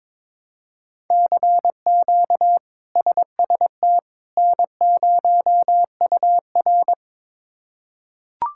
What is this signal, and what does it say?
Morse code, heard as a keyed tone, about 23 wpm, 700 Hz. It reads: CQ SST N0UR
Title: K1USN SST QSO Practice - 22wpm